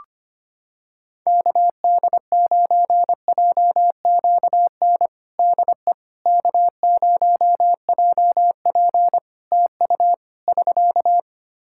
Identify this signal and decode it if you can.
Morse code, heard as a keyed tone, about 25 wpm, 700 Hz. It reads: KD9JQN DE K0JP TU <SK>
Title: Practice QSO - Exchange 4 - Ending - 25wpm